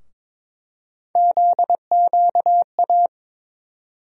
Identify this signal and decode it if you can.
Morse code, heard as a keyed tone, about 22 wpm, 700 Hz. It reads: ZQA
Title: USA Call Sign Postfixes - 3 Letters 22wpm